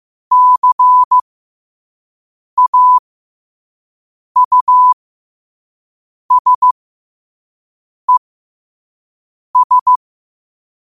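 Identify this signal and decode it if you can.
Morse code, heard as a keyed tone, about 15 wpm, spaced as for 5 wpm, 1000 Hz. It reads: CAUSES
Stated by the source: Morse code practice at 5 wpm.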